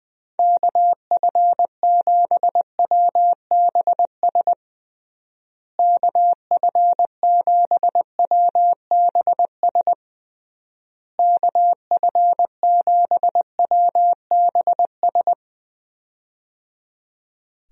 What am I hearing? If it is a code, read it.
Morse code, heard as a keyed tone, about 20 wpm, 700 Hz. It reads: KF7WBS KF7WBS KF7WBS
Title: USA Call Signs - 6 Characters - Repeated 3x - 20wpm